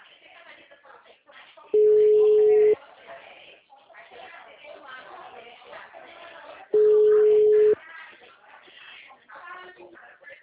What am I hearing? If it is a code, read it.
Ringback tone (Europe, 425 Hz), with speech.
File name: out- -2022- - - .45167